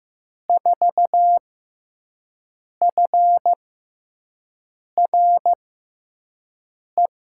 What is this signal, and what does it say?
Morse code, heard as a keyed tone, about 15 wpm, spaced as for 5 wpm, 700 Hz. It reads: 4FRE
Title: USA Call Signs - 6x Character Spacing - 15wpm